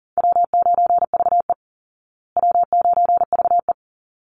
Morse code, heard as a keyed tone, about 40 wpm, 700 Hz. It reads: W9VE W9VE
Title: Easy Call Signs Repeated 2x - 40wpm